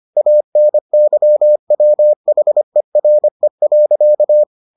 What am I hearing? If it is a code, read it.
Morse code, heard as a keyed tone, about 25 wpm, 600 Hz. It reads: ANYWHERE.